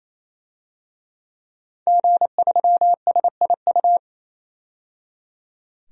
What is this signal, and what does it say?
Morse code, heard as a keyed tone, about 27 wpm, 700 Hz. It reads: G3SIU
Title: DXCC-100 Call Signs - 5 Characters - 28wpm